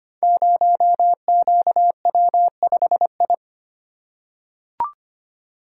Morse code, heard as a keyed tone, about 25 wpm, 700 Hz. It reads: 0QW5I
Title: Code Groups - 5 Letter:Numbers - 25wpm